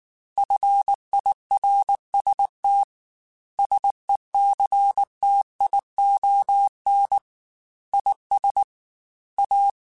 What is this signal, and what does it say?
Morse code, heard as a keyed tone, about 19 wpm, 800 Hz. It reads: FIRST SECTION IS A
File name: Technical Subjects / Technical Subjects-mp3s / Sw Pwr Sup @18